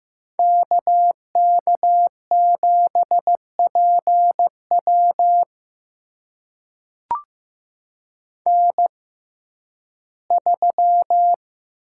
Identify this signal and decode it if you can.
Morse code, heard as a keyed tone, about 15 wpm, spaced as for 11 wpm, 700 Hz. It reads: KK7PW N3
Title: USA Call Signs - 5 Characters - 6x Character Spacing - 15wpm